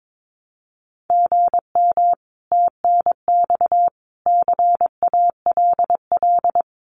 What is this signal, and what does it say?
Morse code, heard as a keyed tone, about 22 wpm, 700 Hz. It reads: GM TNX CALL
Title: Common QSO Phrases 22wpm